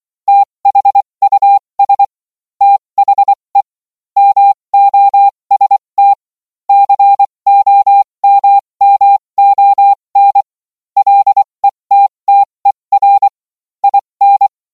Morse code, heard as a keyed tone, about 24 wpm, spaced as for 20 wpm, 800 Hz. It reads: THUS THE MOST COMMON LETTER IN